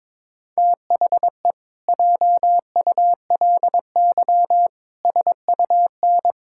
Morse code, heard as a keyed tone, about 22 wpm, 700 Hz. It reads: THE JULY SUN
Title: Pangram Challenge v2 - 22wpm